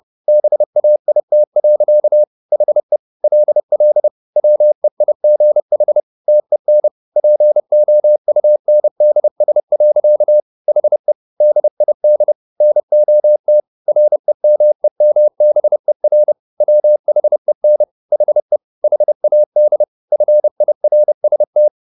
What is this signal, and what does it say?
Morse code, heard as a keyed tone, about 30 wpm, 600 Hz. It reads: BAIT. HE LL WEIGH TEN POUNDS. HE DID NOT REMEMBER WHEN HE HAD FIRST
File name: the old man and the sea / the old man and the sea 1 30wpm